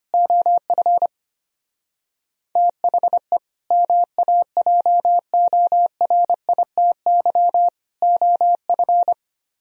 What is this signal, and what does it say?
Morse code, heard as a keyed tone, about 27 wpm, 700 Hz. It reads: OF THE MAJORITY OF
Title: Speed-Racing - Sets of 3 Words 25wpm